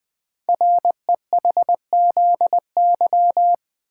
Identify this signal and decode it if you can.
Morse code, heard as a keyed tone, about 20 wpm, 700 Hz. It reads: REHZY
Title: Code Groups - 5 Letters - 20wpm